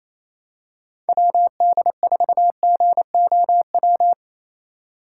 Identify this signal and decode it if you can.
Morse code, heard as a keyed tone, about 28 wpm, 700 Hz. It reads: WD4GOW
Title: USA Call Signs - 6 Characters - 2x Character Spacing - 28wpm